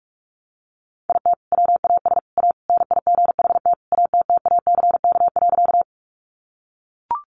Morse code, heard as a keyed tone, about 44 wpm, 700 Hz. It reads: IT WAS A NIGHT ATTACK.
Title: Sentences from Top 1500 Words 45wpm